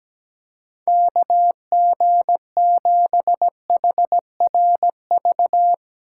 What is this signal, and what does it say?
Morse code, heard as a keyed tone, about 17 wpm, 700 Hz. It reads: KG7HRV
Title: USA Call Signs - 6 Characters - 3x Character Spacing - 17wpm